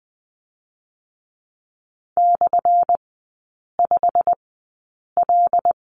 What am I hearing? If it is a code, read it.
Morse code, heard as a keyed tone, about 20 wpm, spaced as for 8 wpm, 700 Hz. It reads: /5L
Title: 178 Lesson 29e - Slash - Three Characters 20wpm